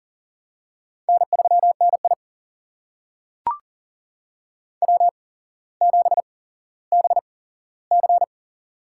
Morse code, heard as a keyed tone, about 40 wpm, spaced as for 17 wpm, 700 Hz. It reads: N3NI W7BC
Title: USA Call Signs - 4 Characters - 8x Character Spacing - 40wpm